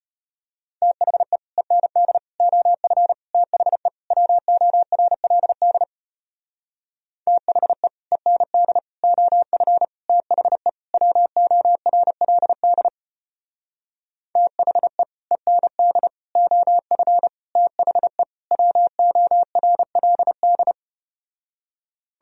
Morse code, heard as a keyed tone, about 33 wpm, 700 Hz. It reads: THE END OF THE WORLD THE END OF THE WORLD THE END OF THE WORLD